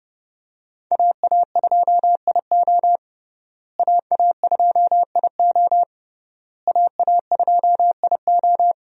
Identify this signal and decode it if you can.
Morse code, heard as a keyed tone, about 30 wpm, 700 Hz. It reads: AA2IO AA2IO AA2IO
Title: USA Call Signs - 5 Characters - Repeated 3x - 30wpm